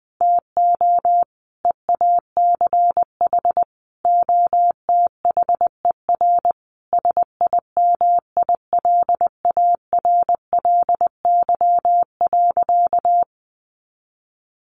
Morse code, heard as a keyed tone, about 20 wpm, 700 Hz. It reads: TO EACH OTHER SIMILARLY.
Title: Sentences from Top 1400 Words 20wpm